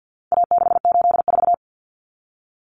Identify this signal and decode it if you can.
Morse code, heard as a keyed tone, about 48 wpm, 700 Hz. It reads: A684
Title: Common Cut Numbers - 4-Digit 50wpm